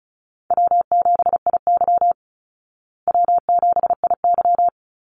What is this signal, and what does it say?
Morse code, heard as a keyed tone, about 34 wpm, 700 Hz. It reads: W7IY W7IY
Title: Easy Call Signs Repeated 3x - 35wpm